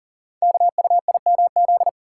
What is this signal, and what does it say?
Morse code, heard as a keyed tone, about 40 wpm, 690 Hz. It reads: KUIMZ
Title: Code Groups - 5 Letters - 40wpm